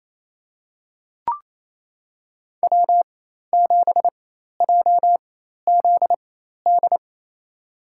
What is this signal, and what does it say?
Morse code, heard as a keyed tone, about 28 wpm, spaced as for 13 wpm, 700 Hz. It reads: W7JZD
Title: USA Call Signs - 5 Characters - 4x Character Spacing - 28wpm